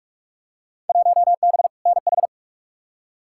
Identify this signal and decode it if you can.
Morse code, heard as a keyed tone, about 46 wpm, 700 Hz. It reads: 1B NH
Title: Field Day Exchanges Encore - 45wpm